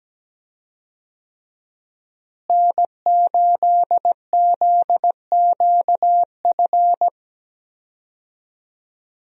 Morse code, heard as a keyed tone, about 17 wpm, 700 Hz. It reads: N8ZQF
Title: USA Call Signs - 5 Characters - Repeated 3x - 17wpm